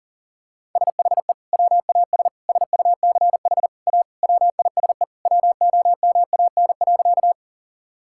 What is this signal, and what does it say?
Morse code, heard as a keyed tone, about 40 wpm, 700 Hz. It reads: SHE WAS SUCH A WISE WOMAN.
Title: Sentences from Top 1600 Words 40wpm